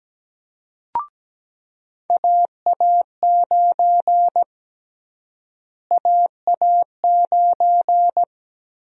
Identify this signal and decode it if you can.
Morse code, heard as a keyed tone, about 17 wpm, 700 Hz. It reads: AA9 AA9
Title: USA Call Sign Prefixes - Repeated 2x - 17wpm